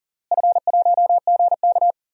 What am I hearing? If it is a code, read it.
Morse code, heard as a keyed tone, about 37 wpm, 700 Hz. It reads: F1GK
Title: DXCC-100 Call Signs - 4 Characters - Repeated 3x - 40wpm